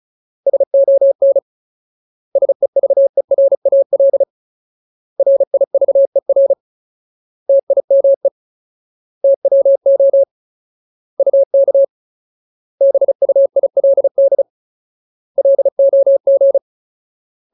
Morse code, heard as a keyed tone, about 34 wpm, 550 Hz. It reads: SON SEVERAL RIVER TIME TWO UK BUILD LOG
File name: Ultimate Word List / 35WPM Ultimate